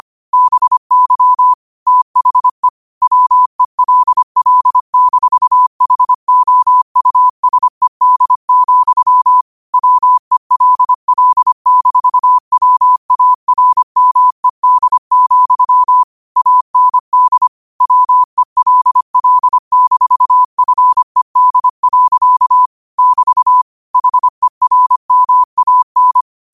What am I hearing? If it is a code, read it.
Morse code, heard as a keyed tone, about 25 wpm, 1000 Hz. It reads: BY THE WELL-HOUSED, WELL-WARMED, AND WELL-FED. <BT> HERMAN